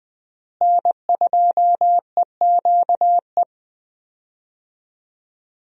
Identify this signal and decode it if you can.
Morse code, heard as a keyed tone, about 20 wpm, 700 Hz. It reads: N2EQE
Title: USA Call Signs - 5 Characters - Repeated 2x - 20wpm